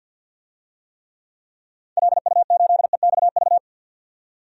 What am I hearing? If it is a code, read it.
Morse code, heard as a keyed tone, about 52 wpm, 700 Hz. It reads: LU8EXV
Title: DXCC-100 Call Signs - 6 Characters - 50wpm